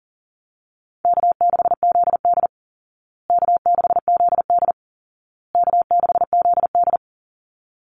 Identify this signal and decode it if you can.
Morse code, heard as a keyed tone, about 40 wpm, 700 Hz. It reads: K6ZD K6ZD K6ZD
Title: Call Signs Repeated 2x - 40wpm